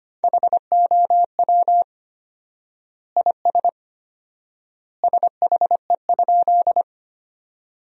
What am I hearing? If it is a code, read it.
Morse code, heard as a keyed tone, about 25 wpm, 700 Hz. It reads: HOW IS SHE?
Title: 4x Word-Spacing - Sentences from Top 100 Words - 25wpm